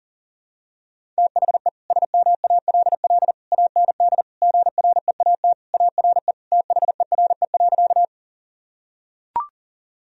Morse code, heard as a keyed tone, about 40 wpm, 700 Hz. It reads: THE SMALL AND GREAT ARE THERE.
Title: Sentences from Top 1000 Words - Review Part 1 - 40wpm